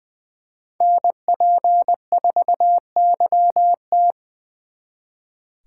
Morse code, heard as a keyed tone, about 23 wpm, 700 Hz. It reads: NP4YT
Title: Speed Racing - USA 2x2 Call Signs 20wpm